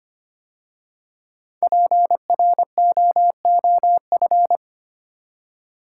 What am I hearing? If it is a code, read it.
Morse code, heard as a keyed tone, about 25 wpm, 700 Hz. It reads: PROOF
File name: 5 Letter Words - First 3 Letters Given - 25wpm